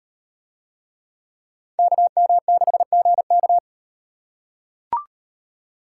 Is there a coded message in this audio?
Yes: Morse code from 1.8 to 3.6 s, heard as a keyed tone, about 35 wpm, 700 Hz. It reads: KM6GK